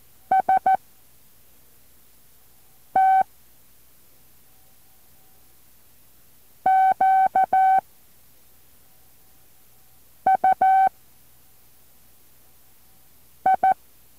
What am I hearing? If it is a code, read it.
Morse code, heard as a keyed tone, about 14 wpm, 760 Hz. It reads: S T Q U I
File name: intl morse code2